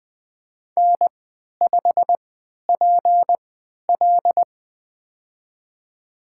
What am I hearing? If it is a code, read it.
Morse code, heard as a keyed tone, about 20 wpm, spaced as for 11 wpm, 700 Hz. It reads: N5PL
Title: USA Call Signs - 4 Characters - 3x Character Spacing - 20wpm